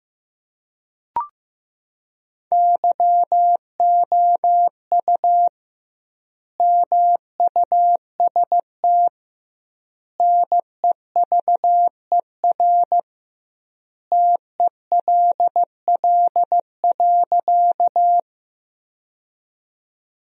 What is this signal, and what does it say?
Morse code, heard as a keyed tone, about 15 wpm, 700 Hz. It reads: YOU MUST NEVER TELL.
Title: 2x Word-Spacing - Sentences from Top 300 Words - 15wpm